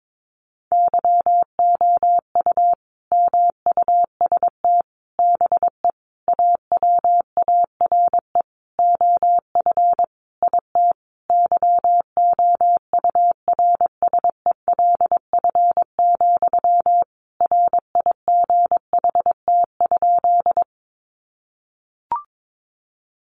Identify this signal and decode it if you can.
Morse code, heard as a keyed tone, about 22 wpm, 700 Hz. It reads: YOU MUST BE AWARE OF IT YOURSELF, RIGHT?
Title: Sentences from Top 1200 Words 22wpm